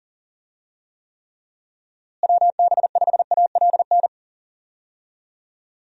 Morse code, heard as a keyed tone, about 40 wpm, 700 Hz. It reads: WB5ALN
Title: USA Call Signs - 6 Characters - Repeated 2x - 40wpm